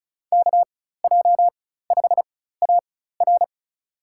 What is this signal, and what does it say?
Morse code, heard as a keyed tone, about 34 wpm, spaced as for 16 wpm, 700 Hz. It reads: KJ5AR
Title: USA Call Signs - 5 Characters - 4x Character Spacing - 35wpm